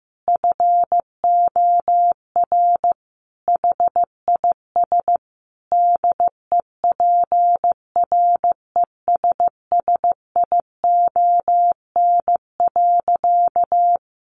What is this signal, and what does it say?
Morse code, heard as a keyed tone, about 15 wpm, 700 Hz. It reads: FOR HIS DEPRESSION.
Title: Sentences from Top 1300 Words 15wpm